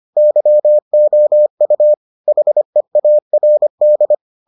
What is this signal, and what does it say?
Morse code, heard as a keyed tone, about 25 wpm, 600 Hz. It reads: YOU HEARD